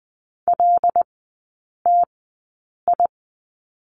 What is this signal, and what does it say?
Morse code, heard as a keyed tone, about 20 wpm, spaced as for 8 wpm, 690 Hz. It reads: LTI
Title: 081 Lesson 14e - Number 2 - Three Characters 20wpm